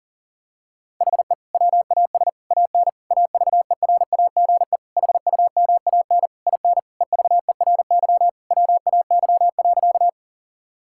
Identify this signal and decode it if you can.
Morse code, heard as a keyed tone, about 37 wpm, 700 Hz. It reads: HE WAS AN AVERAGE HUMAN IN EVERY WAY.
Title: Sentences from Top 1000 Words - Review Part 6 - 40wpm